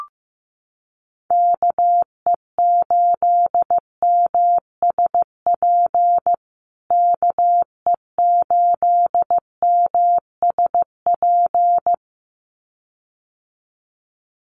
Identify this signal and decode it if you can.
Morse code, heard as a keyed tone, about 15 wpm, 700 Hz. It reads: KE8MSP KE8MSP
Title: USA Call Signs Repeated 2x 15wpm